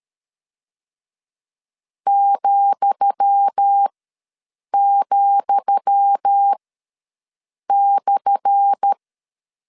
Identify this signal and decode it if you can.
Morse code, heard as a keyed tone, about 13 wpm, spaced as for 6 wpm, 800 Hz. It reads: ,,/
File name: LEZIONI / 10 SIGNS / 10D COMA ws 12wpm 3618